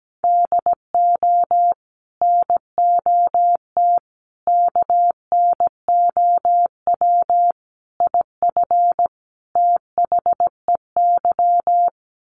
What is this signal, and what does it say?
Morse code, heard as a keyed tone, about 17 wpm, 700 Hz. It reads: DO NOT KNOW IF THEY
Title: Sets of 5 Words Encore 17wpm